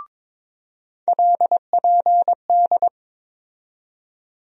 Morse code, heard as a keyed tone, about 22 wpm, 700 Hz. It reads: LPD
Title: USA Call Sign Postfixes - 3 Letters 22wpm